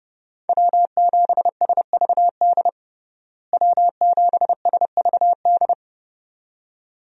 Morse code, heard as a keyed tone, about 30 wpm, 700 Hz. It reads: W7SVD W7SVD
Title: USA Call Signs - 5 Characters - Repeated 2x - 30wpm